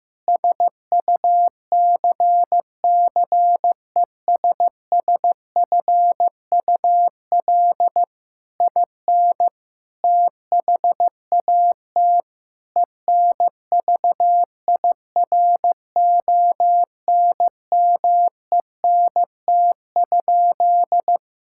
Morse code, heard as a keyed tone, about 15 wpm, 700 Hz. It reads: SUCCESSFUL IN THAT ENVIRONMENT?
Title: Sentences from Top 1000 Words - Review Part 5 - 15wpm